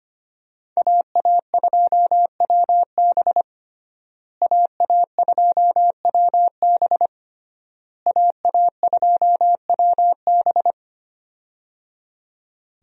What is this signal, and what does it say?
Morse code, heard as a keyed tone, about 25 wpm, 700 Hz. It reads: AA2WB AA2WB AA2WB
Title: USA Call Signs - 5 Characters - Repeated 3x - 25wpm